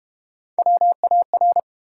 Morse code, heard as a keyed tone, about 30 wpm, 690 Hz. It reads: WAR
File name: Easy Words - Speed-Racing with Hints - 25wpm